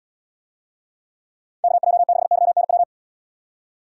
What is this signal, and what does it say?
Morse code, heard as a keyed tone, about 75 wpm, 700 Hz. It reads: KJ6YNW